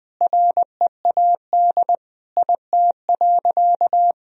Morse code, heard as a keyed tone, about 20 wpm, 700 Hz. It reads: READ IT.
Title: Sentences from Top 1000 Words - Review Part 1 - 20wpm